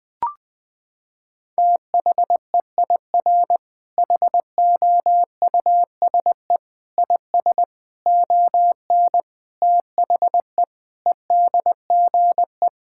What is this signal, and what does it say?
Morse code, heard as a keyed tone, about 20 wpm, 700 Hz. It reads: THEIR HOUSE IS ON THE EDGE
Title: Sentences with 5-Word Phrases - 20wpm